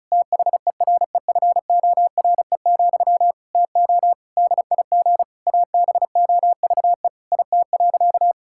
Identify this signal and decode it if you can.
Morse code, heard as a keyed tone, about 34 wpm, 700 Hz. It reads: THEREFORE, TO DIG ABOVE IT.
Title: Sentences from Top 1600 Words 35wpm